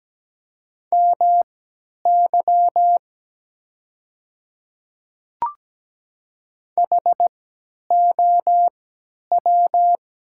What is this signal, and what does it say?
Morse code, heard as a keyed tone, about 17 wpm, spaced as for 10 wpm, 700 Hz. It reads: MY HOW
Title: 3x Character Spacing - Top 100 Words - 17wpm